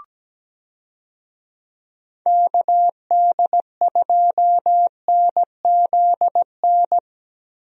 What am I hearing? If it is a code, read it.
Morse code, heard as a keyed tone, about 17 wpm, 700 Hz. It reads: KD2NZN
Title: USA Call Signs - 6 Characters - Repeated 2x - 17wpm